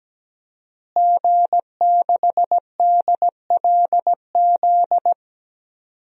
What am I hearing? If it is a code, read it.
Morse code, heard as a keyed tone, about 17 wpm, 700 Hz. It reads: G6DLZ
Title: DXCC-100 Call Signs - 5 Characters - Repeated 3x - 17wpm